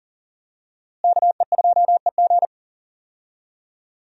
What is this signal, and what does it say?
Morse code, heard as a keyed tone, about 40 wpm, 700 Hz. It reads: KE2EG
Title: USA Call Signs - 5 Characters - Repeated 2x - 40wpm